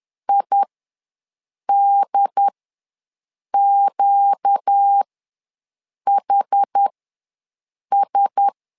Morse code, heard as a keyed tone, about 10 wpm, spaced as for 6 wpm, 800 Hz. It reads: IDQHS
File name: LEZIONI / 4 Q1S ws 10wpm 2208